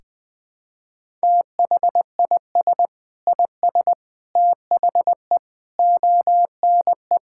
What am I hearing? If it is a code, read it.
Morse code, heard as a keyed tone, about 21 wpm, 700 Hz. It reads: THIS IS THE ONE
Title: Sentences from Top 1000 Words - Review Part 5 - 20wpm